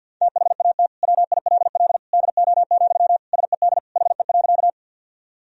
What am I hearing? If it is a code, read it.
Morse code, heard as a keyed tone, about 51 wpm, 700 Hz. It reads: THAT WILL DO, SED HE.
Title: Sentences from Top 100 Words and Abbreviations 50wpm